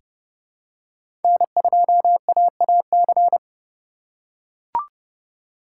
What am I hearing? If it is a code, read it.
Morse code, heard as a keyed tone, about 32 wpm, 700 Hz. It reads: N2AAC
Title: USA Call Signs - 4x Character Spacing - 30wpm